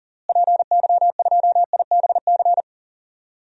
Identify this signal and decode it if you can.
Morse code, heard as a keyed tone, about 40 wpm, 700 Hz. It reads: PY2IBC
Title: DXCC-100 Call Signs - 6 Characters - Repeated 3x - 40wpm